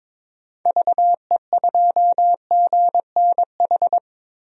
Morse code, heard as a keyed tone, about 22 wpm, 700 Hz. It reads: VE2GNH
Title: DXCC-100 Call Signs - 6 Characters - Repeated 3x - 22wpm